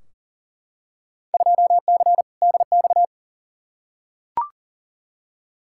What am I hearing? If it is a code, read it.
Morse code, heard as a keyed tone, about 40 wpm, 700 Hz. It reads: 2C DX
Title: Field Day Exchanges Encore - 40wpm